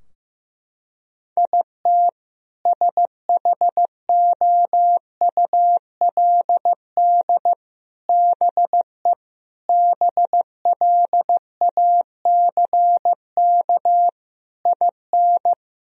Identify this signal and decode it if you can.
Morse code, heard as a keyed tone, about 16 wpm, 700 Hz. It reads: IT SHOULD BE BLACK IN
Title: Sentences from Top 1000 Words - Review Part 2 - 15wpm